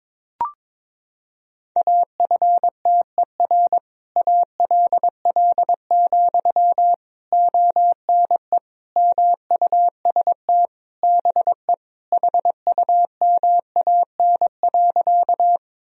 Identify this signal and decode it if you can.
Morse code, heard as a keyed tone, about 22 wpm, 700 Hz. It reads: AFTER ALL, ONE MUST BE HUMAN.
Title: Sentences from Top 1000 Words - Review Part 3 - 22wpm